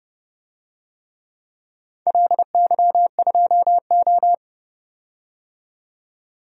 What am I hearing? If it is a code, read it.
Morse code, heard as a keyed tone, about 31 wpm, 700 Hz. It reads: LY2O
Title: DXCC-100 Call Signs - 4 Characters - 30wpm